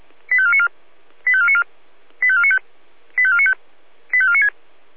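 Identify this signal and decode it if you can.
Five-tone selective call (ZVEI), heard as quick runs of five tones, tones of about 70 ms. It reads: 85494 85494 85495 85495 85496
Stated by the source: ZVEI five-tone selective call received by SDR.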